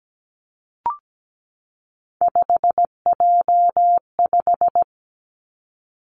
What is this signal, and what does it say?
Morse code, heard as a keyed tone, about 17 wpm, 700 Hz. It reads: 5J5
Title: Three Challenging Letter-Number Combinations 17wpm